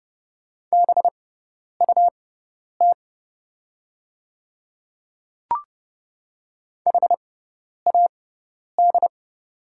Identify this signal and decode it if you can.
Morse code, heard as a keyed tone, about 30 wpm, spaced as for 10 wpm, 690 Hz. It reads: BUT HAD
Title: 6x Character Spacing - Top 200 Words - 30wpm